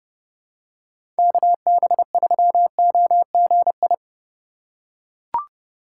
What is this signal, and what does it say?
Morse code, heard as a keyed tone, about 31 wpm, 700 Hz. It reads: KB3OGI
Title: USA Call Signs - 6 Characters - 4x Character Spacing - 30wpm